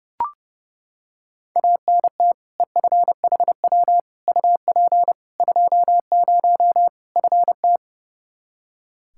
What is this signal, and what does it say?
Morse code, heard as a keyed tone, about 29 wpm, 700 Hz. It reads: ANT EFHW UP 20 FT
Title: Practice QSO - Exchange 2 - Info - 30wpm